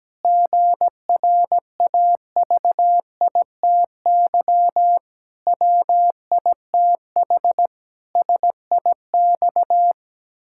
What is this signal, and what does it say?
Morse code, heard as a keyed tone, about 17 wpm, 700 Hz. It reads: GRAVITY WITH SIX
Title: Pangram Challenge v2 - 17wpm